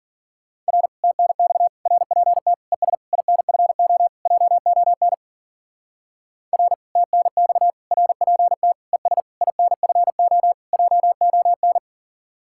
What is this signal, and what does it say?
Morse code, heard as a keyed tone, about 48 wpm, 700 Hz. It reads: R TNX RPT ES INFO JON R TNX RPT ES INFO JON
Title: Speed-Racing Common QSO Phrases Encore - 40wpm